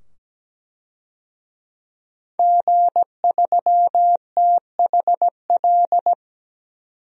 Morse code, heard as a keyed tone, about 17 wpm, 700 Hz. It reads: G3THL
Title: DXCC-100 Call Signs - 5 Characters - 17wpm